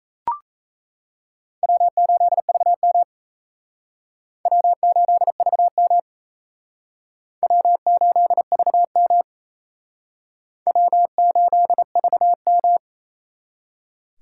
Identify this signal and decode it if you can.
Morse code, heard as a keyed tone, about 34 wpm, 700 Hz. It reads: W8VM W8VM W8VM W8VM
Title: Speed Racing - USA 1x2 Call Signs 28wpm